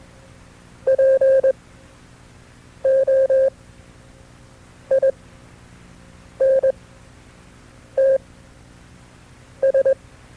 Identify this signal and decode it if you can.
Morse code, heard as a keyed tone, about 21 wpm, spaced as for 6 wpm, 540 Hz. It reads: POINTS
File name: FreeFiles / CW-practice / KB6MT-tape2-side2